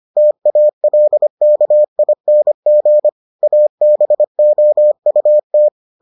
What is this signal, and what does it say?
Morse code, heard as a keyed tone, about 25 wpm, 600 Hz. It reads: TALKING ABOUT